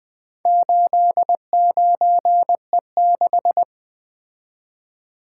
Morse code, heard as a keyed tone, about 20 wpm, 700 Hz. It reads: 89E6
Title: Four Letter-Number Combinations - 20wpm